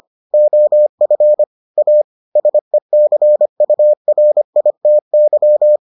Morse code, heard as a keyed tone, about 25 wpm, 600 Hz. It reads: OF A SECURITY